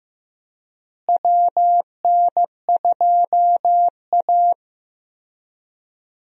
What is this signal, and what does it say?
Morse code, heard as a keyed tone, about 15 wpm, 700 Hz. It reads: WN2A
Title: USA Call Signs - 4 Characters - Repeated 2x - 15wpm